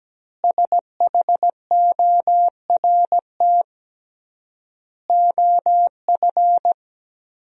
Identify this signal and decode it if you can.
Morse code, heard as a keyed tone, about 17 wpm, 700 Hz. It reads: SHORT OF
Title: 3x Word-Spacing - Sentences from Top 400 Words - 17wpm